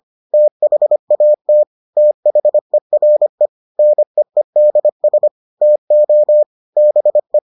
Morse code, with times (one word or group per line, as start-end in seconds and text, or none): 0.3-1.6 THAT
2.0-3.5 THERE
3.8-5.3 NEEDS
5.6-6.4 TO
6.8-7.4 BE